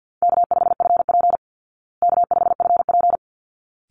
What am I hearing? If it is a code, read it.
Morse code, heard as a keyed tone, about 50 wpm, 700 Hz. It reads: K5FP K5FP
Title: Easy Call Signs Repeated 2x - 50wpm